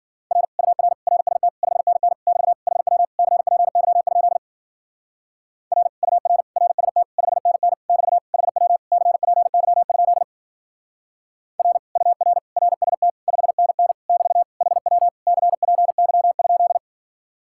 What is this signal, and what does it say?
Morse code, heard as a keyed tone, about 47 wpm, 700 Hz. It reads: R UR RST 5NN <BT> HW CPY? R UR RST 5NN <BT> HW CPY? R UR RST 5NN <BT> HW CPY?
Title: Speed-Racing Common QSO Phrases - 40wpm